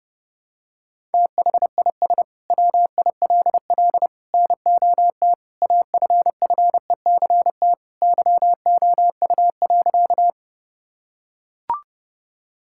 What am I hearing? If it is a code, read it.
Morse code, heard as a keyed tone, about 30 wpm, 700 Hz. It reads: THIS WILL NOT AFFECT YOU.
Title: Sentences from Top 1000 Words - Review Part 6 - 30wpm